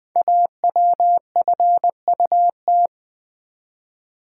Morse code, heard as a keyed tone, about 20 wpm, 700 Hz. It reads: AWFUT
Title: Code Groups - 5 Letters - 20wpm